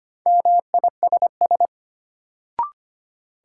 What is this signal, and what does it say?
Morse code, heard as a keyed tone, about 25 wpm, 700 Hz. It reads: MISS
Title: Sentences from Top 500 Words - New Word Drill - 25wpm